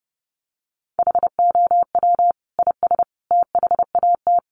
Morse code, heard as a keyed tone, about 30 wpm, 700 Hz. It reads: HOW IS THAT
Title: Sentences from Top 500 Words 30wpm